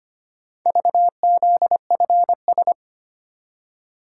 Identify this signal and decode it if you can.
Morse code, heard as a keyed tone, about 25 wpm, 700 Hz. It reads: VZFS
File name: practice / 25wpm / Four Letter-Number Combinations - 25wpm